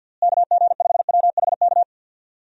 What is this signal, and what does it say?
Morse code, heard as a keyed tone, about 46 wpm, 700 Hz. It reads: KG5WHK
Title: USA Call Signs - 6 Characters - Repeated 2x - 50wpm